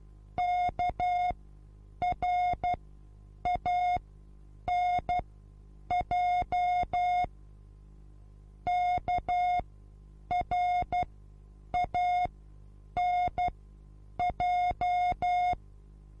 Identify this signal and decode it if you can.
Morse code, heard as a keyed tone, about 12 wpm, spaced as for 8 wpm, 720 Hz. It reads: KRANJ KRANJ
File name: repos / Morsekurs / DJ4UF Morsen23